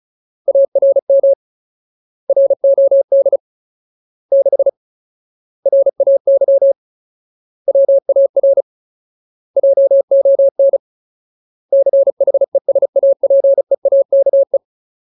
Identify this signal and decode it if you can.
Morse code, heard as a keyed tone, about 34 wpm, 550 Hz. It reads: ARM ROD 6 RAY WAR JON CHESAPEAKE
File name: Ultimate Word List / 35WPM Ultimate